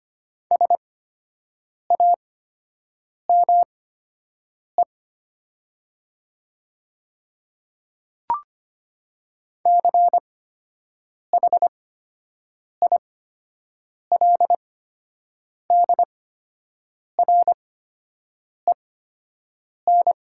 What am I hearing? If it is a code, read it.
Morse code, heard as a keyed tone, about 25 wpm, spaced as for 7 wpm, 700 Hz. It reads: SAME CHILDREN
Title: 8x Character Spacing - Top 300 Words - 25wpm